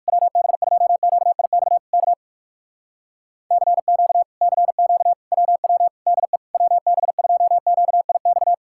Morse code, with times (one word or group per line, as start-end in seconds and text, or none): 0.1-1.8 WB2QIX
1.9-2.1 K
3.5-4.2 CQ
4.4-5.1 CQ
5.3-5.9 WW
6.1-6.4 DE
6.5-8.5 WB2QIX